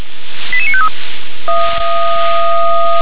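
Five-tone selective call (ZVEI), heard as quick runs of five tones, tones of about 70 ms. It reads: 88953
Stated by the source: ZVEI five-tone selective call received on an SDR.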